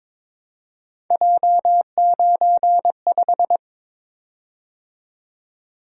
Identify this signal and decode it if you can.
Morse code, heard as a keyed tone, about 22 wpm, 700 Hz. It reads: J95
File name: practice / 22wpm / Three Letter-Number Combinations - 22wpm